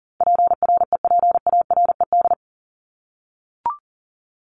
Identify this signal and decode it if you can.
Morse code, heard as a keyed tone, about 40 wpm, 700 Hz. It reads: PREPARED
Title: ED Suffix - Top 100 Words with Base Words 40wpm